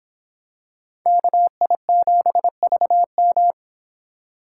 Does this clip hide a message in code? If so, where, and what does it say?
Yes: Morse code from 1.1 to 3.5 s, heard as a keyed tone, about 25 wpm, 700 Hz. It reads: KI7VM